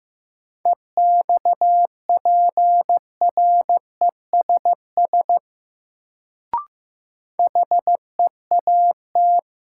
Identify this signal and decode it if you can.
Morse code, heard as a keyed tone, about 15 wpm, 700 Hz. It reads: EXPRESS HEAT
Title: Sentences from Top 1100 Words - New Word Drill Alt - 15wpm